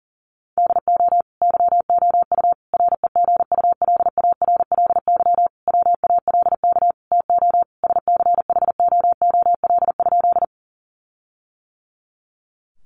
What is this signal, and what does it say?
Morse code, heard as a keyed tone, about 37 wpm, 700 Hz. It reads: DO YOU REGULARLY WALK TO SCHOOL?
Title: Sentences from Top 1400 Words 40wpm